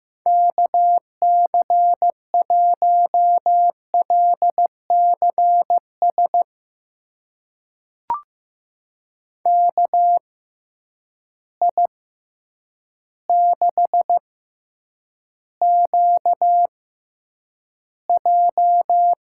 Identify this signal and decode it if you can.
Morse code, heard as a keyed tone, about 15 wpm, spaced as for 8 wpm, 700 Hz. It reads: KC1LCS KI6QJ
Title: USA Call Signs - 6 Characters - 6x Character Spacing - 15wpm